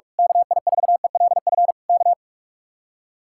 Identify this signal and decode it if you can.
Morse code, heard as a keyed tone, about 45 wpm, 700 Hz. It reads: KI4ELF K
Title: Speed-Racing Common QSO Phrases Encore - 45wpm